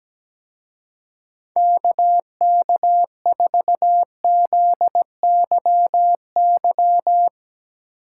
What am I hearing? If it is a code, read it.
Morse code, heard as a keyed tone, about 18 wpm, 700 Hz. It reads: KK4ZYY
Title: Speed Racing - USA 2x3 Call Signs 17wpm